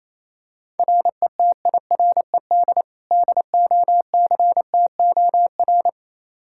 Morse code, heard as a keyed tone, about 28 wpm, 700 Hz. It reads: RETIRED DOCTOR
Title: Practice QSO - Exchange 3 - More Info - 28wpm